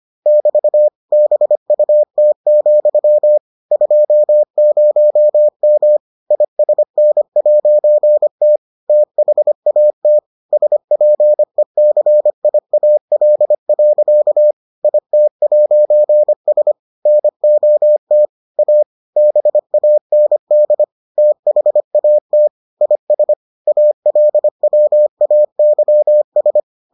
Morse code, heard as a keyed tone, about 25 wpm, 600 Hz. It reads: <BT> BUT, 20M ISN'T THAT SPECIAL. IT'S NOT A BAND THAT IS ALWAYS